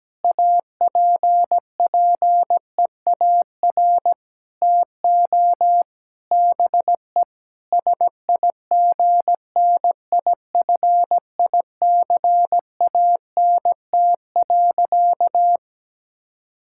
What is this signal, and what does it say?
Morse code, heard as a keyed tone, about 17 wpm, 700 Hz. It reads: APPEAR TO BE SIGNIFICANT.
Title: Sentences from Top 1000 Words - Review Part 6 - 17wpm